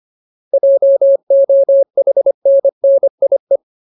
Morse code, heard as a keyed tone, about 25 wpm, 550 Hz. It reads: JOHNNIE
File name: Ultimate Word List / 25WPM Ultimate